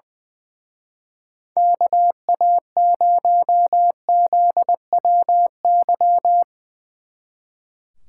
Morse code, heard as a keyed tone, about 21 wpm, 700 Hz. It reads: KA0ZWY
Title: Speed Racing - USA Call Signs 20wpm